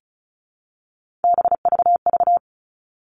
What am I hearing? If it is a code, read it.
Morse code, heard as a keyed tone, about 34 wpm, 700 Hz. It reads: BVV
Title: All Cut Numbers - 3-Digit 35wpm